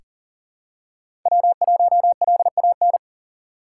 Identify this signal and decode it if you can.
Morse code, heard as a keyed tone, about 40 wpm, 700 Hz. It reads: W1LAN
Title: USA Call Signs - 3x Character Spacing - 40wpm